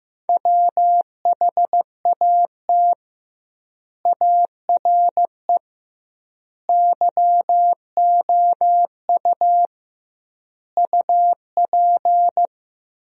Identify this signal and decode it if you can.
Morse code, heard as a keyed tone, about 15 wpm, 700 Hz. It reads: WHAT ARE YOU UP
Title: 2x Word-Spacing - Sentences from Top 100 Words - 15wpm